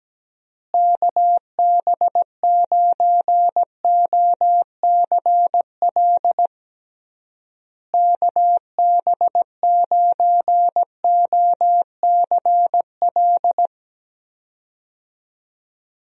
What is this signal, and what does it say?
Morse code, heard as a keyed tone, about 17 wpm, 700 Hz. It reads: KB9OCL KB9OCL
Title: USA Call Signs - 6 Characters - Repeated 3x - 17wpm